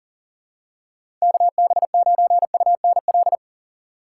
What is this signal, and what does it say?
Morse code, heard as a keyed tone, about 37 wpm, 700 Hz. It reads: KB9UNL
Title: USA Call Signs - 6 Characters - 3x Character Spacing - 40wpm